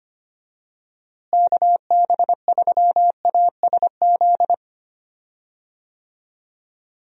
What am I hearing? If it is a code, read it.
Morse code, heard as a keyed tone, about 25 wpm, 700 Hz. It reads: KB3ASZ
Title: USA Call Signs - 6 Characters - Repeated 2x - 25wpm